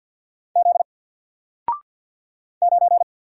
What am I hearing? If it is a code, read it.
Morse code, heard as a keyed tone, about 54 wpm, 700 Hz. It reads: Z 9
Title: Single Letter-Number Alt 50wpm